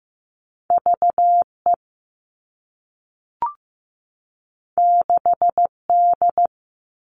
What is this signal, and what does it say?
Morse code, heard as a keyed tone, about 15 wpm, 700 Hz. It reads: VE 6D
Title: All Cut Numbers - 2-Digit 15wpm